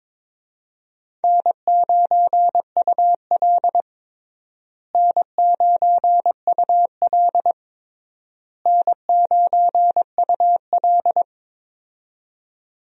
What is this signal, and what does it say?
Morse code, heard as a keyed tone, about 22 wpm, 700 Hz. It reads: N9UL N9UL N9UL
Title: USA Call Signs - 4 Characters - Repeated 3x - 22wpm